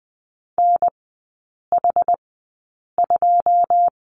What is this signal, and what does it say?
Morse code, heard as a keyed tone, about 20 wpm, spaced as for 8 wpm, 700 Hz. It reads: NH2
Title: 114 Lesson 19e - Letter w - Three Characters 20wpm